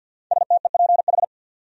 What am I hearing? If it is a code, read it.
Morse code, heard as a keyed tone, about 53 wpm, 700 Hz. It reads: STEPH
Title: USA Common Names - Max 5 Letters - 50wpm